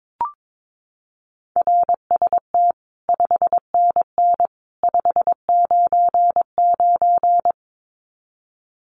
Morse code, heard as a keyed tone, about 22 wpm, 700 Hz. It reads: RST 5NN 599
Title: Common QSO Phrases 22wpm